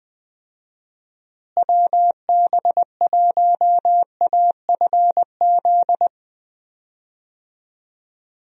Morse code, heard as a keyed tone, about 21 wpm, 700 Hz. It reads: WB1AFZ
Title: USA Call Signs - 6 Characters - Repeated 3x - 20wpm